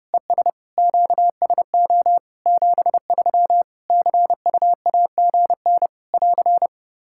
Morse code, heard as a keyed tone, about 32 wpm, 700 Hz. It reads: ES QSO 73 CUAGN <AR>
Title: K1USN SST QSO Practice 30wpm